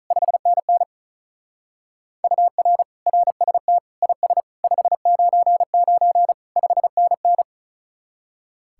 Morse code, heard as a keyed tone, about 42 wpm, 700 Hz. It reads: 5NN UR RST IS 599 5NN
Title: Speed-Racing Common QSO Phrases Encore - 35wpm